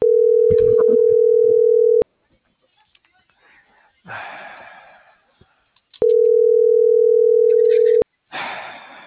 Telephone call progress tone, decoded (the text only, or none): ringback tone